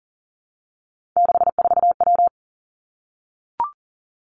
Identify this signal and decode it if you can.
Morse code, heard as a keyed tone, about 40 wpm, 700 Hz. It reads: 64W
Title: Three Challenging Letter-Number Combinations 40wpm